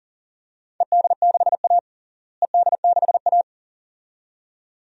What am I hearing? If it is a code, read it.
Morse code, heard as a keyed tone, about 40 wpm, 700 Hz. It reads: ED6A ED6A
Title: DXCC-100 Call Signs - 4 Characters - Repeated 2x - 40wpm